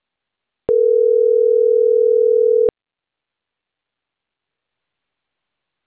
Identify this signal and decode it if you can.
Ringback tone (North America), 440 + 480 Hz.